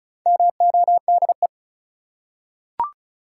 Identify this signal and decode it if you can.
Morse code, heard as a keyed tone, about 34 wpm, 700 Hz. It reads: MODE
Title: Four Letter Words - 35wpm